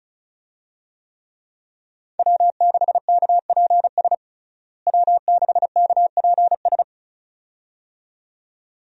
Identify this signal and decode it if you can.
Morse code, heard as a keyed tone, about 36 wpm, 700 Hz. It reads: W6KPS W6KPS
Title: USA Call Signs - 5 Characters - Repeated 2x - 35wpm